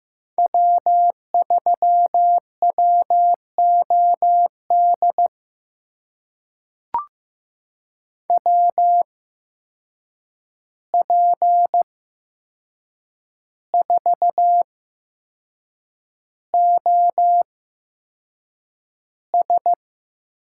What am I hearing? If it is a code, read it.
Morse code, heard as a keyed tone, about 15 wpm, spaced as for 6 wpm, 700 Hz. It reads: W3WOD WP4OS
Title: USA Call Signs - 5 Characters - 8x Character Spacing - 15wpm